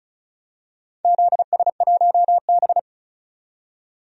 Morse code, heard as a keyed tone, about 36 wpm, 700 Hz. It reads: ZS1B
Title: DXCC-100 Call Signs - 4 Characters - 35wpm